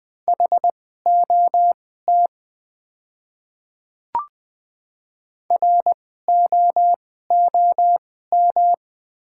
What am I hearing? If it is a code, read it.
Morse code, heard as a keyed tone, about 20 wpm, spaced as for 14 wpm, 700 Hz. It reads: HOT ROOM
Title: 2x Character Spacing - Top 300 Words - 20wpm